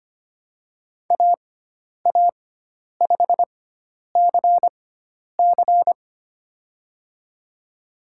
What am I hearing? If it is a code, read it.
Morse code, heard as a keyed tone, about 25 wpm, spaced as for 10 wpm, 700 Hz. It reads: AA5CC
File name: Lesson 16g - Letter c - Call Signs 25wpm